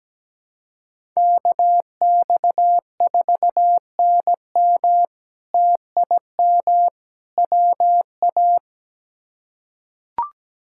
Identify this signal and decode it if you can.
Morse code, heard as a keyed tone, about 17 wpm, 700 Hz. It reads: KX4NM TIM WA
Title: K1USN SST QSO Practice - 17wpm